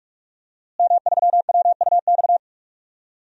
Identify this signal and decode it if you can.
Morse code, heard as a keyed tone, about 45 wpm, 700 Hz. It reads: M3WUX
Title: DXCC-100 Call Signs - 5 Characters - Repeated 3x - 45wpm